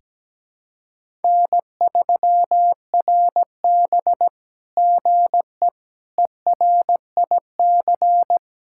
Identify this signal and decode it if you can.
Morse code, heard as a keyed tone, about 17 wpm, 700 Hz. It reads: N3RB GE ERIC
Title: SOTA QSO Practice 17wpm